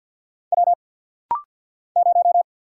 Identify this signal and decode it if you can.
Morse code, heard as a keyed tone, about 48 wpm, 700 Hz. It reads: W 0
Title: Single Letter-Number Alt - Rapid-Fire 50wpm